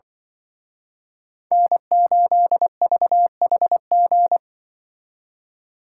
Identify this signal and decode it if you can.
Morse code, heard as a keyed tone, about 24 wpm, 700 Hz. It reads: N8VHG
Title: Speed Racing - USA 1x3 Call Signs 20wpm